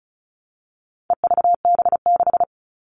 Morse code, heard as a keyed tone, about 34 wpm, 700 Hz. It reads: EVB6
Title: All Cut Numbers - 4-Digit 35wpm